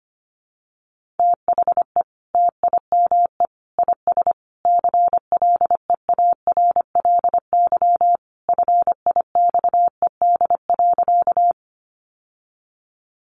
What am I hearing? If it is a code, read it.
Morse code, heard as a keyed tone, about 25 wpm, 700 Hz. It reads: THE TIME IS CLEARLY FIXED.
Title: Sentences from Top 1300 Words 25wpm